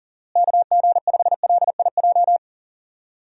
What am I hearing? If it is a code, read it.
Morse code, heard as a keyed tone, about 40 wpm, 700 Hz. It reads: KG5LIJ
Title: USA Call Signs - 6 Characters - Repeated 3x - 40wpm